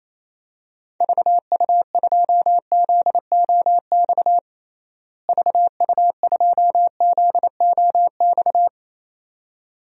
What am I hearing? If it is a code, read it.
Morse code, heard as a keyed tone, about 28 wpm, 700 Hz. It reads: VU2ZOX VU2ZOX
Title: DXCC-100 Call Signs - 6 Characters - Repeated 2x - 28wpm